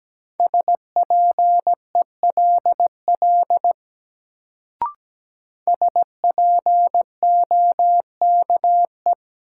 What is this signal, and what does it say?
Morse code, heard as a keyed tone, about 17 wpm, 700 Hz. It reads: SPELL SPOKE
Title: Sentences from Top 1200 Words - New Word Drill Alt - 17wpm